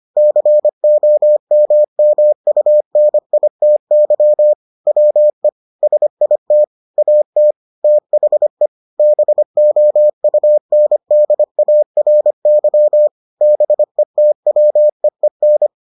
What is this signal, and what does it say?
Morse code, heard as a keyed tone, about 25 wpm, 600 Hz. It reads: COMMUNITY WE SIT AT THE BOUNDARY BETWEEN